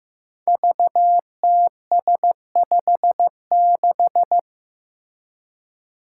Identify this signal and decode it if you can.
Morse code, heard as a keyed tone, about 15 wpm, 700 Hz. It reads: VTS56
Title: Five Challenging Letter-Number Combinations 15wpm